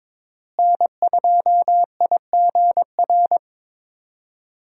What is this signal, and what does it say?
Morse code, heard as a keyed tone, about 22 wpm, 700 Hz. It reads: N2IGR
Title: USA Call Signs - 5 Characters - Repeated 2x - 22wpm